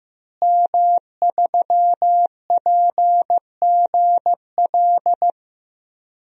Morse code, heard as a keyed tone, about 15 wpm, 700 Hz. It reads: M3PGL
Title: DXCC-100 Call Signs - 5 Characters - Repeated 2x - 15wpm